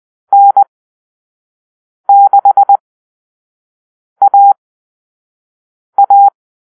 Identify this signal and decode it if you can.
Morse code, heard as a keyed tone, about 20 wpm, spaced as for 5 wpm, 800 Hz. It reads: N6AA
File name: Call-Sign Drill